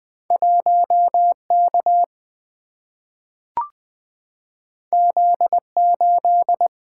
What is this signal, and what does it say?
Morse code, heard as a keyed tone, about 20 wpm, 700 Hz. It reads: 1 K Z 8
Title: Single-Letter Single-Number 20wpm